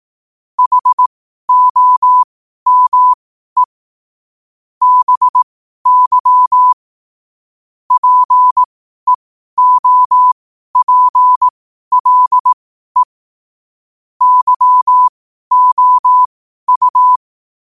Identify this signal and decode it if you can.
Morse code, heard as a keyed tone, about 18 wpm, spaced as for 12 wpm, 1000 Hz. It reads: HOME BY PEOPLE YOU
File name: qotdfiles / 12wpm